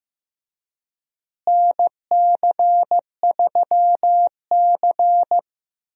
Morse code, heard as a keyed tone, about 15 wpm, 700 Hz. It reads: NC3C
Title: USA Call Signs - 4 Characters - Repeated 3x - 15wpm